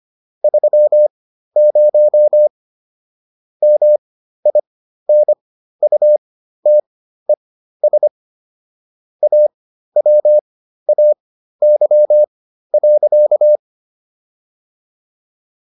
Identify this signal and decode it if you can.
Morse code, heard as a keyed tone, about 25 wpm, spaced as for 13 wpm, 600 Hz. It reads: 30 MINUTES AWAY.